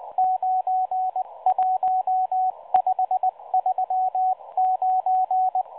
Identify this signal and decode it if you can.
Morse code, heard as a keyed tone, about 19 wpm, 750 Hz, with radio noise. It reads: 91539
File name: HM02 5-14-15 0140 AM UTC 7351kHz CW